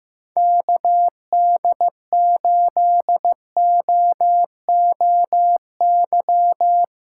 Morse code, heard as a keyed tone, about 15 wpm, 700 Hz. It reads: KD8OOY
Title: USA Callsign DE Callsign - 6 Characters - 15wpm